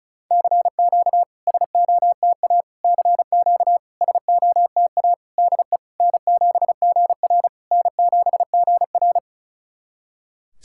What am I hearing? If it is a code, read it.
Morse code, heard as a keyed tone, about 36 wpm, 690 Hz. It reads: CQ SOTA CQ SOTA DE N7GR N7GR
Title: SOTA QSO Practice 35wpm